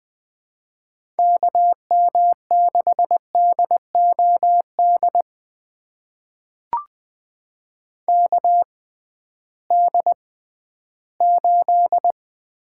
Morse code, heard as a keyed tone, about 21 wpm, spaced as for 13 wpm, 700 Hz. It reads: KM6DOD KD8
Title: USA Call Signs - 6 Characters - 6x Character Spacing - 20wpm